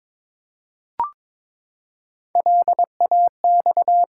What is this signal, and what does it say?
Morse code, heard as a keyed tone, about 22 wpm, 700 Hz. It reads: LAX
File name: ARRL Sections v2 - 22wpm